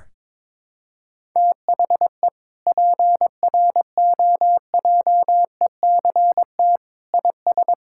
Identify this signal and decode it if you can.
Morse code, heard as a keyed tone, about 22 wpm, 700 Hz. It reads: THE PROJECT IS
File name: practice / 22wpm / Sentences with 8-Word Phrases - 22wpm